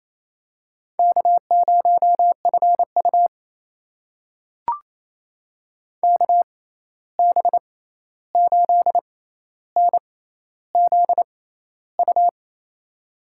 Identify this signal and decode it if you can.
Morse code, heard as a keyed tone, about 28 wpm, spaced as for 13 wpm, 700 Hz. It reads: K0FU KB8NZU
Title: USA Call Signs - 6x Character Spacing - 28wpm